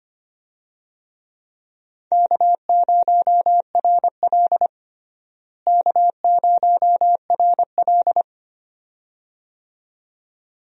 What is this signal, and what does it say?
Morse code, heard as a keyed tone, about 25 wpm, 700 Hz. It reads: K0RL K0RL
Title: USA Call Signs - 4 Characters - Repeated 2x - 25wpm